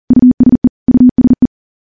Morse code, heard as a keyed tone, about 42 wpm, 250 Hz. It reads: USE USE
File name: morse-training-40WPM